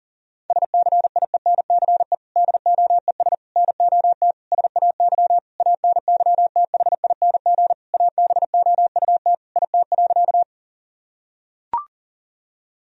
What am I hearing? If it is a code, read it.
Morse code, heard as a keyed tone, about 40 wpm, 700 Hz. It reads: SCIENCE DOES NOT SAY ANYTHING ABOUT IT.
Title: Sentences from Top 1000 Words - Review Part 2 - 40wpm